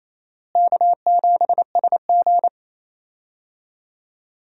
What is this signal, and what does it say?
Morse code, heard as a keyed tone, about 28 wpm, 700 Hz. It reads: K7SG
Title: USA Call Signs - 4 Characters - Repeated 3x - 28wpm